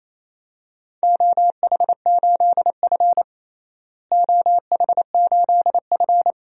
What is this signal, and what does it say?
Morse code, heard as a keyed tone, about 28 wpm, 700 Hz. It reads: OH8F OH8F
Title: DXCC-100 Call Signs - 4 Characters - Repeated 3x - 28wpm